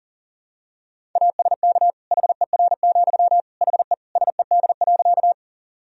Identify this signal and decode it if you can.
Morse code, heard as a keyed tone, about 37 wpm, 700 Hz. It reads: ASK HER, HE SED.
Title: Sentences from Top 200 Words and Abbreviations 40wpm